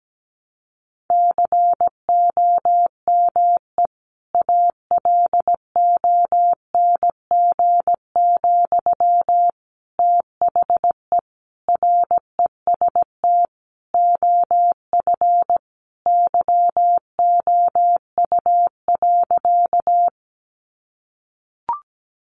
Morse code, heard as a keyed tone, about 17 wpm, 700 Hz. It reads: COME ALONG, THE REST OF YOU.
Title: Sentences from Top 700 Words 17wpm